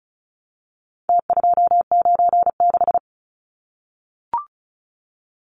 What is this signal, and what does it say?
Morse code, heard as a keyed tone, about 34 wpm, 700 Hz. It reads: T296
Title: Numbers - Contest Style - 4-Digit 35wpm